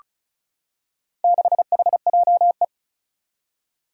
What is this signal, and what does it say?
Morse code, heard as a keyed tone, about 36 wpm, 700 Hz. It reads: 6HJE
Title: Four Challenging Letter-Number Combinations 35wpm